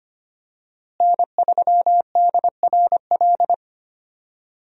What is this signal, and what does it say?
Morse code, heard as a keyed tone, about 25 wpm, 700 Hz. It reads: N3DRL
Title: USA Call Signs - 5 Characters - Repeated 2x - 25wpm